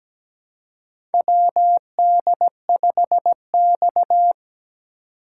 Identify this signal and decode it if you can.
Morse code, heard as a keyed tone, about 17 wpm, 700 Hz. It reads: WD5X
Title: USA Call Signs - 4 Characters - 3x Character Spacing - 17wpm